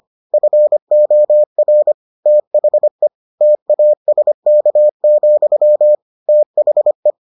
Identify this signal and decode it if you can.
Morse code, heard as a keyed tone, about 25 wpm, 600 Hz. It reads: FOR THE TASK, THE